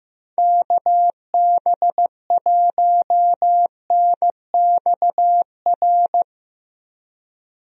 Morse code, heard as a keyed tone, about 16 wpm, 700 Hz. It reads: KB1NXR
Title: USA Call Signs - 6 Characters - Repeated 3x - 15wpm